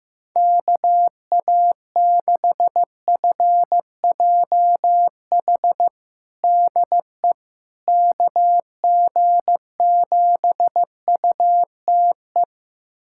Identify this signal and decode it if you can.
Morse code, heard as a keyed tone, about 15 wpm, 700 Hz. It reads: KA6FJH DE KG7UTE
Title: USA Callsign DE Callsign - 6 Characters - 15wpm